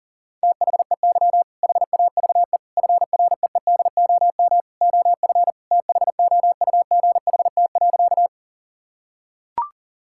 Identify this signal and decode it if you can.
Morse code, heard as a keyed tone, about 40 wpm, 700 Hz. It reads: THEY HAVE FREEDOM OF THOUGHT.
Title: Sentences from Top 1000 Words - Review Part 5 - 40wpm